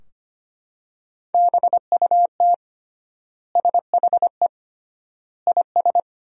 Morse code, heard as a keyed tone, about 25 wpm, 700 Hz. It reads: BUT SHE IS
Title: 3x Word-Spacing - Sentences from Top 400 Words - 25wpm